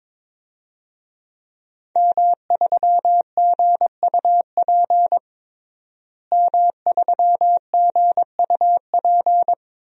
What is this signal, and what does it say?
Morse code, heard as a keyed tone, about 21 wpm, 700 Hz. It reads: M3GUP M3GUP
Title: DXCC-100 Call Signs - 5 Characters - Repeated 2x - 22wpm